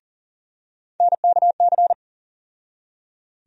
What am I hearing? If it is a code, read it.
Morse code, heard as a keyed tone, about 40 wpm, 700 Hz. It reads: NKC
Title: USA Call Sign Postfixes - 3 Letters 40wpm